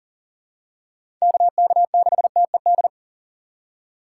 Morse code, heard as a keyed tone, about 40 wpm, 700 Hz. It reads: KK6TED
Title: USA Call Signs - 6x Character Spacing - 40wpm